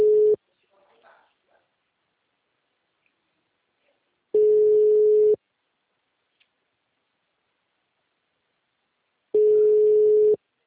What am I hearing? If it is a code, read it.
Ringback tone (Europe, 425 Hz).